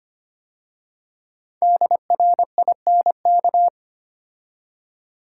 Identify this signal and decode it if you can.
Morse code, heard as a keyed tone, about 25 wpm, 700 Hz. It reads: DRINK
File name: 5 Letter Words - First 3 Letters Given - 25wpm